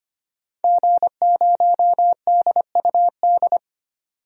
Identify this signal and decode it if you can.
Morse code, heard as a keyed tone, about 25 wpm, 700 Hz. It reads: G0DUD
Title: DXCC-100 Call Signs - 5 Characters - 25wpm